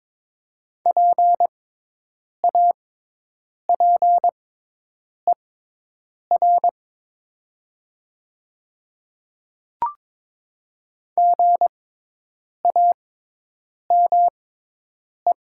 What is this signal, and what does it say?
Morse code, heard as a keyed tone, about 22 wpm, spaced as for 8 wpm, 700 Hz. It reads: PAPER GAME
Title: 6x Character Spacing - Top 500 Words - 22wpm